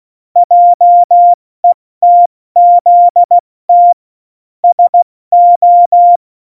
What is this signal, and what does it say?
Morse code, heard as a keyed tone, about 16 wpm, 700 Hz. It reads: JETZT SO
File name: Morsekurs / Witze / Witz 15. September 2020, 70 F80